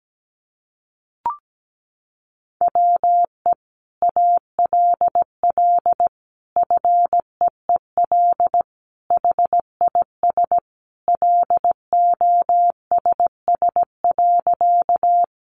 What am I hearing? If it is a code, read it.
Morse code, heard as a keyed tone, about 17 wpm, 700 Hz. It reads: WE ALL FEEL HIS LOSS.
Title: Sentences from Top 1400 Words 17wpm